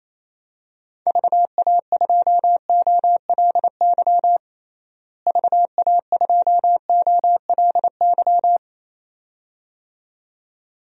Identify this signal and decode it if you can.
Morse code, heard as a keyed tone, about 28 wpm, 700 Hz. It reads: VA2OLY VA2OLY
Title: DXCC-100 Call Signs - 6 Characters - Repeated 2x - 28wpm